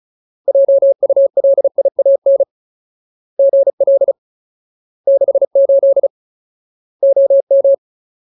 Morse code, heard as a keyed tone, about 34 wpm, 550 Hz. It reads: JULIAN GL 68 OM
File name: Ultimate Word List / 35WPM Ultimate